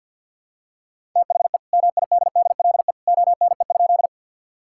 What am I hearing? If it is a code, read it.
Morse code, heard as a keyed tone, about 50 wpm, 700 Hz. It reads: THE MIDDLE ONE?
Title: Sentences from Top 1000 Words - Review Part 6 - 50wpm